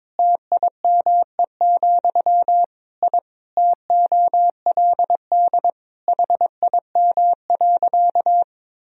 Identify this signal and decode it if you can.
Morse code, heard as a keyed tone, about 22 wpm, 700 Hz. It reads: TIME, I TOLD HIM.
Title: Sentences from Top 1000 Words - Review Part 1 - 22wpm